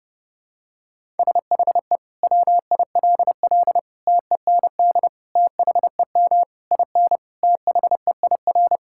Morse code, heard as a keyed tone, about 30 wpm, 700 Hz. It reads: SHE WILL TEND THEM IN THEIR
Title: Sentences from Top 1000 Words - Review Part 5 - 30wpm